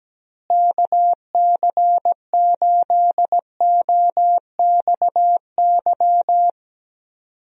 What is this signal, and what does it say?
Morse code, heard as a keyed tone, about 17 wpm, 700 Hz. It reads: KC8OXY
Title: USA Call Signs - 6 Characters - Repeated 3x - 17wpm